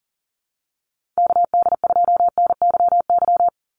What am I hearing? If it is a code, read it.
Morse code, heard as a keyed tone, about 40 wpm, 690 Hz. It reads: KD2NYY
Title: USA Call Signs 40wpm